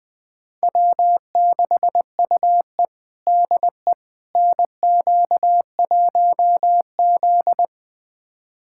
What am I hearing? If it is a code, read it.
Morse code, heard as a keyed tone, about 20 wpm, 700 Hz. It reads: W6UE DE NQ1Z
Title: USA Callsign DE Callsign - 4 Characters - 20wpm